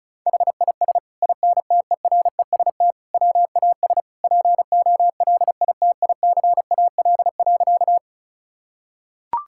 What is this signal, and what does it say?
Morse code, heard as a keyed tone, about 34 wpm, 690 Hz. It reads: HIS INTEREST WAS POLITICAL.
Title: Sentences from Top 1000 Words - Review Part 5 - 35wpm